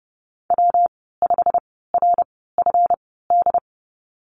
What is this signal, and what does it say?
Morse code, heard as a keyed tone, about 30 wpm, spaced as for 17 wpm, 700 Hz. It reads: W5RFD
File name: Lesson 37g - Letter x - Call Signs 30wpm